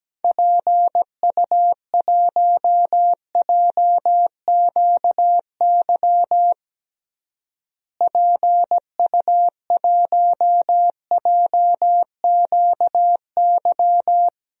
Morse code, heard as a keyed tone, about 17 wpm, 700 Hz. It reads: PU1JQY PU1JQY
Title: DXCC-100 Call Signs - 6 Characters - Repeated 3x - 17wpm